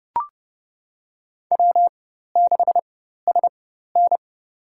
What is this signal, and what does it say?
Morse code, heard as a keyed tone, about 30 wpm, spaced as for 14 wpm, 700 Hz. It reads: W6SN
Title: USA Call Signs - 4 Characters - 4x Character Spacing - 30wpm